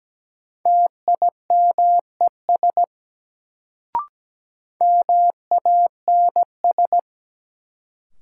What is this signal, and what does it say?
Morse code, heard as a keyed tone, about 17 wpm, 700 Hz. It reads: TIMES MANS
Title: Sentences from Top 1000 Words - Review Part 6 - New Word Drill - 17wpm